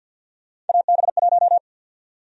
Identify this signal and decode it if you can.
Morse code, heard as a keyed tone, about 53 wpm, 700 Hz. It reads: AB1
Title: USA Call Sign Prefixes - Repeated 2x - 50wpm